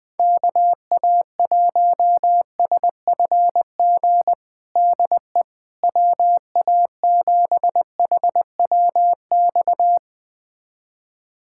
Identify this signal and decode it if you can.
Morse code, heard as a keyed tone, about 20 wpm, 700 Hz. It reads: KA1SFG DE WA7HWX
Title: USA Callsign DE Callsign 20wpm